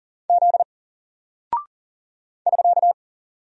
Morse code, heard as a keyed tone, about 40 wpm, 700 Hz. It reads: Z <SK>
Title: Single Character (Letter:Number:Punctuation:Prosign) 40wpm